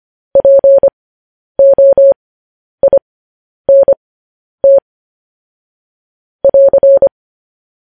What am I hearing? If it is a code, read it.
Morse code, heard as a keyed tone, about 25 wpm, spaced as for 10 wpm, 550 Hz. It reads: POINT <AR>
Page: news headlines in morse code at 10 wpm